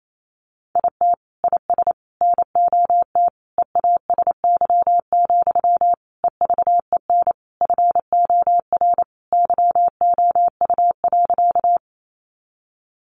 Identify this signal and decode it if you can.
Morse code, heard as a keyed tone, about 28 wpm, 700 Hz. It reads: IT IS NOT EASY, EVEN FOR YOU.
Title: Sentences from Top 700 Words 28wpm